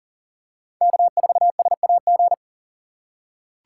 Morse code, heard as a keyed tone, about 40 wpm, 700 Hz. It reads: K4SAG
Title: USA Call Signs - 4x Character Spacing - 40wpm